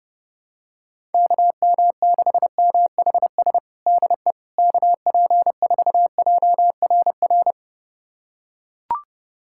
Morse code, heard as a keyed tone, about 29 wpm, 700 Hz. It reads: KM6MHS DE KP4JRR
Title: USA Callsign DE Callsign - 6 Characters - 30wpm